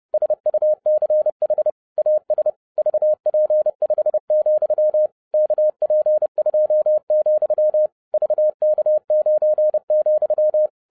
Morse code, heard as a keyed tone, about 30 wpm, 610 Hz. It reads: SUCH AS VP5, KP2, VK9,
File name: 30wpm